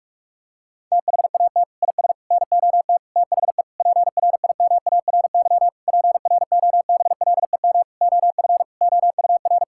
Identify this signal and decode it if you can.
Morse code, heard as a keyed tone, about 46 wpm, 700 Hz. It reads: THAT IS NOT THE PRIMARY PROBLEM OF OUR
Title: Sentences from Top 1000 Words - Review Part 5 - 45wpm